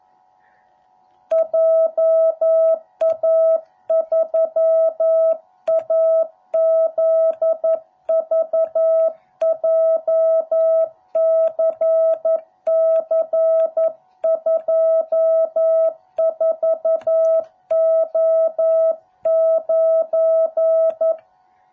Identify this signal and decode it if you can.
Morse code, heard as a keyed tone, about 11 wpm, 660 Hz. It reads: JA3AZV JCC24O9